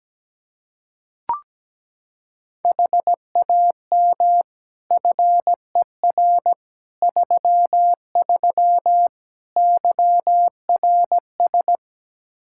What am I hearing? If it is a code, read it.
Morse code, heard as a keyed tone, about 17 wpm, 700 Hz. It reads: HAM FER 33 YRS
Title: Practice QSO - Exchange 3 - More Info - 17wpm